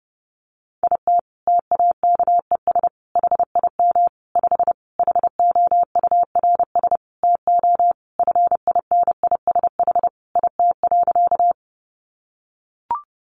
Morse code, heard as a keyed tone, about 30 wpm, 700 Hz. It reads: IT TAKES HIM 5 HOURS TO FINISH IT.
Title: Additional Easy Phrases 30wpm